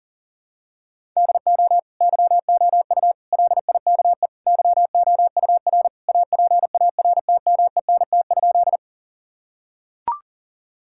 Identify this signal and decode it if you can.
Morse code, heard as a keyed tone, about 40 wpm, 700 Hz. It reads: DO YOU LIKE YOUR APARTMENT?
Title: Sentences from Top 1000 Words - Review Part 6 - 40wpm